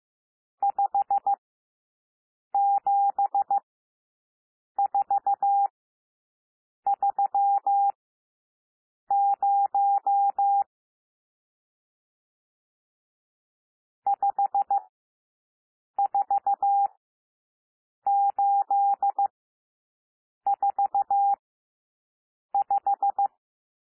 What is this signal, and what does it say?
Morse code, heard as a keyed tone, about 15 wpm, spaced as for 6 wpm, 800 Hz. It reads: 57430 54845
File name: cwmp3 / N1 SOUND NUMBERS 15 wpm